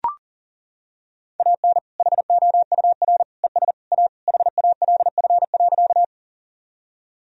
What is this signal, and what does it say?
Morse code, heard as a keyed tone, about 40 wpm, 700 Hz. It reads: AN HOUR ES A HALF.
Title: Sentences from Top 400 Words and Abbreviations 40wpm